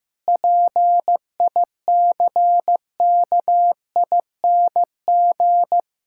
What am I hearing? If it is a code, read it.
Morse code, heard as a keyed tone, about 15 wpm, 700 Hz. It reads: PICKING
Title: Pangram Challenge v2 - 15wpm